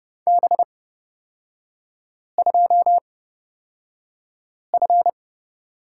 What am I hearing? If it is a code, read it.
Morse code, heard as a keyed tone, about 30 wpm, spaced as for 5 wpm, 700 Hz. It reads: B2F
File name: Lesson 38e - Letter q - Three Characters 30wpm